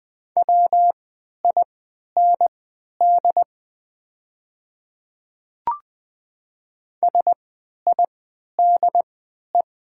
Morse code, heard as a keyed tone, about 20 wpm, spaced as for 11 wpm, 700 Hz. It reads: WIND SIDE
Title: 3x Character Spacing - Top 500 Words - 20wpm